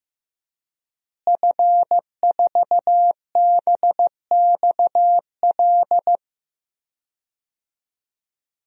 Morse code, heard as a keyed tone, about 15 wpm, 700 Hz. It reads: F4BXL
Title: DXCC-100 Call Signs - 5 Characters - Repeated 3x - 15wpm